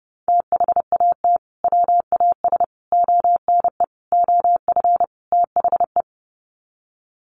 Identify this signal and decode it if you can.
Morse code, heard as a keyed tone, about 30 wpm, 700 Hz. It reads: THAT WAS ONE OF THE
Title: Sets of 5 Words 30wpm